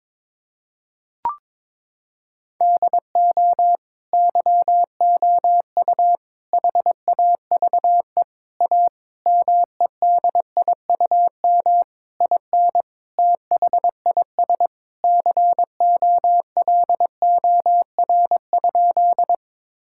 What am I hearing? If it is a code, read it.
Morse code, heard as a keyed tone, about 22 wpm, 700 Hz. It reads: DO YOU HAVE A MEDIUM IN THIS COLOR?
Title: Sentences from Top 1000 Words - Review Part 6 - 22wpm